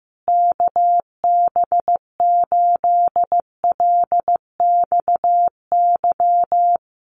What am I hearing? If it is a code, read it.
Morse code, heard as a keyed tone, about 15 wpm, 700 Hz. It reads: KB8LXY
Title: Common QSO Phrases Encore 15wpm